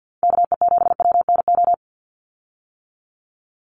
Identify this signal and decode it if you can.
Morse code, heard as a keyed tone, about 50 wpm, 700 Hz. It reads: KE7WNO
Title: USA Call Signs 50wpm